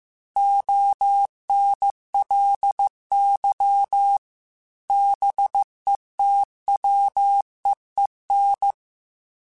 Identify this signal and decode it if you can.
Morse code, heard as a keyed tone, about 15 wpm, 800 Hz. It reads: ONLY BETWEEN
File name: Weather Sets / Weather Sets-mp3s / Weather 4 28 @13